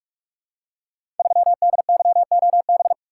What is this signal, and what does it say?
Morse code, heard as a keyed tone, about 44 wpm, 700 Hz. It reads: 3DYOB
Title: Code Groups - 5 Letter:Numbers - 45wpm